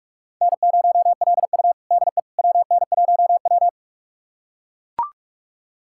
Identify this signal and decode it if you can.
Morse code, heard as a keyed tone, about 44 wpm, 700 Hz. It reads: N0LU DE WN1W
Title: USA Callsign DE Callsign - 4 Characters - 45wpm